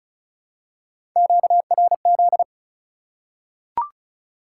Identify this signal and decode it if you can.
Morse code, heard as a keyed tone, about 34 wpm, 700 Hz. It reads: QRZ
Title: Top 180 QSO Elements 35wpm